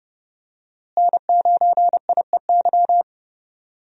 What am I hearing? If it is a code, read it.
Morse code, heard as a keyed tone, about 31 wpm, 700 Hz. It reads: N9IEY
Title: USA Call Signs - 5 Characters - Repeated 2x - 30wpm